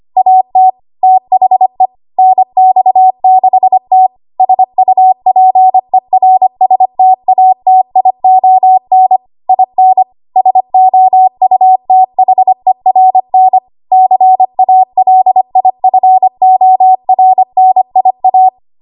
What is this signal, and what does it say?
Morse code, heard as a keyed tone, about 25 wpm, 750 Hz. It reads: AT THE NX6T SUPERSTATION IN SOUTHERN CALIFORNIA
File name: Archive / 25 WPM / 25WPM